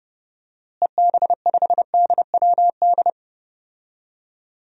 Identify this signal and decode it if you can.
Morse code, heard as a keyed tone, about 30 wpm, 700 Hz. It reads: EB5DWD
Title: DXCC-100 Call Signs - 6 Characters - Repeated 3x - 30wpm